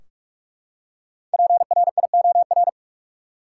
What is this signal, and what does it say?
Morse code, heard as a keyed tone, about 44 wpm, 700 Hz. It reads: PRIOR
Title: Five Letter Words - 45wpm